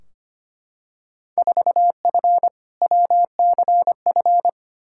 Morse code, heard as a keyed tone, about 25 wpm, 700 Hz. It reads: 4F WCF
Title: Field Day Exchanges - 25wpm